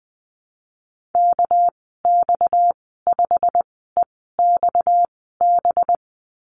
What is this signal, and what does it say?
Morse code, heard as a keyed tone, about 20 wpm, spaced as for 15 wpm, 700 Hz. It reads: KX5EXB
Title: 238 Lesson 38g - Letter q - Call Signs 20wpm